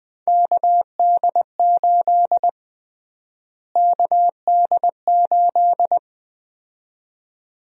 Morse code, heard as a keyed tone, about 20 wpm, 700 Hz. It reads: KD8 KD8
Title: USA Call Sign Prefixes - Repeated 2x - 20wpm